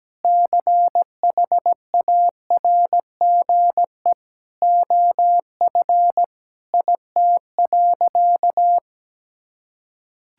Morse code, heard as a keyed tone, about 17 wpm, 700 Hz. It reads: CHARGE OF IT.
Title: Sentences from Top 1000 Words - Review Part 6 - 17wpm